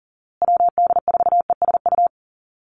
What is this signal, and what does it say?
Morse code, heard as a keyed tone, about 40 wpm, 700 Hz. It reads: WD4ESU
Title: USA Call Signs Repeated 3x 40wpm